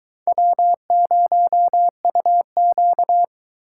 Morse code, heard as a keyed tone, about 20 wpm, 700 Hz. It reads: W0UQ